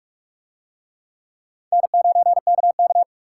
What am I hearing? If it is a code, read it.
Morse code, heard as a keyed tone, about 42 wpm, 700 Hz. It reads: N9KK